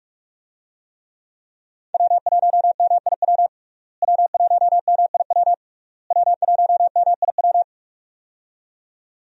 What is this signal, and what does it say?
Morse code, heard as a keyed tone, about 46 wpm, 700 Hz. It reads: W1MIW W1MIW W1MIW
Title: USA Call Signs - 5 Characters - Repeated 3x - 45wpm